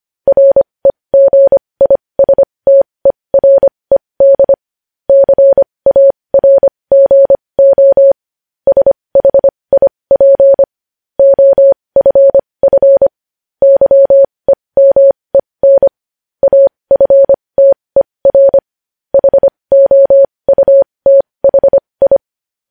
Morse code, heard as a keyed tone, about 25 wpm, spaced as for 20 wpm, 550 Hz. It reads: REGISTERED CARGO SHIP OFF YEMEN AFTER HOUTHI
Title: News Headlines at 20